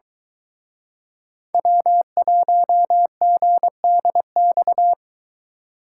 Morse code, heard as a keyed tone, about 23 wpm, 700 Hz. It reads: W1GDX